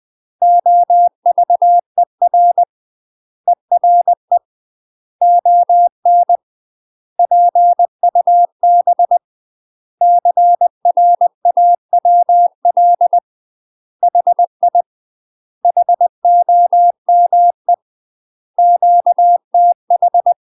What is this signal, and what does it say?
Morse code, heard as a keyed tone, about 20 wpm, 700 Hz. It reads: OVER ERE ON PUB CRAWL HI HOME QTH
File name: UK MASTER QSOs TRAINING / 20WPM / UK MASTER QSOs 0029